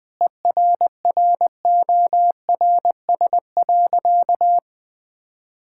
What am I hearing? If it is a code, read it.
Morse code, heard as a keyed tone, about 20 wpm, 700 Hz. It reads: ERRORS.
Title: Difficult Sentences with Hints - 20wpm